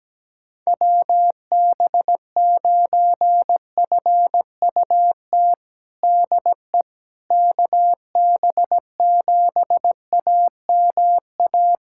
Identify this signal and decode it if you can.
Morse code, heard as a keyed tone, about 17 wpm, 700 Hz. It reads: WB9FUT DE KB7AMA
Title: USA Callsign DE Callsign - 6 Characters - 17wpm